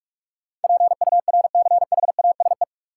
Morse code, heard as a keyed tone, about 45 wpm, 700 Hz. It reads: PURCHASE
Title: Sentences from Top 800 Words - New Word Drill - 45wpm